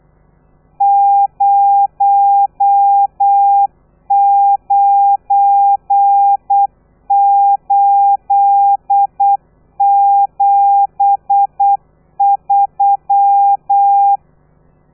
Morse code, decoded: 09873